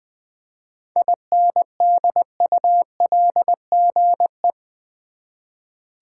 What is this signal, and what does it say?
Morse code, heard as a keyed tone, about 20 wpm, 700 Hz. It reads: INDULGE
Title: 7 Letter Words - First 3 Letters Given - 20wpm